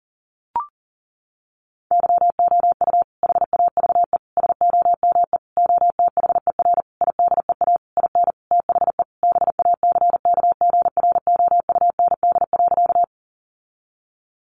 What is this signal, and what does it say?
Morse code, heard as a keyed tone, about 37 wpm, 700 Hz. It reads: YOU HAVE SOME OTHER IDEA IN THE BACKGROUND.
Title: Sentences from Top 1300 Words 40wpm